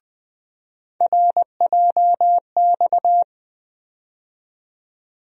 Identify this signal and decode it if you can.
Morse code, heard as a keyed tone, about 20 wpm, 700 Hz. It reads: RJX
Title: USA Call Sign Postfixes - 3 Letters 20wpm